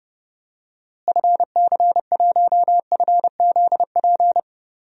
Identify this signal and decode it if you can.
Morse code, heard as a keyed tone, about 30 wpm, 700 Hz. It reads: FC1FZP
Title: DXCC-100 Call Signs - 6 Characters - Repeated 3x - 30wpm